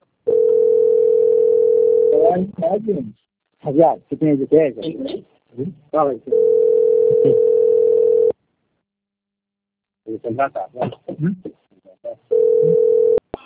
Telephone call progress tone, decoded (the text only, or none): ringback tone